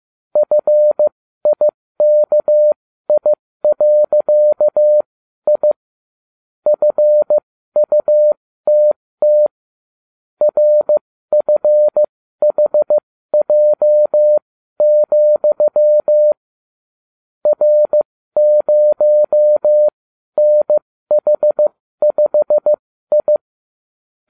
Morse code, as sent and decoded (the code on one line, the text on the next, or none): ..-. .. -.- .. .-.-.- .. / ..-. ..- - - / .-. ..-. .... .--- --..-- / .-. ----- -. .... ..... ..
FIKI.I FUTT RFHJ, R0NH5I